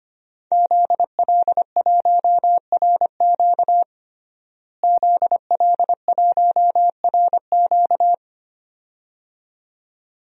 Morse code, heard as a keyed tone, about 25 wpm, 700 Hz. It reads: ZL1RQ ZL1RQ
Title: DXCC-100 Call Signs - 5 Characters - Repeated 2x - 25wpm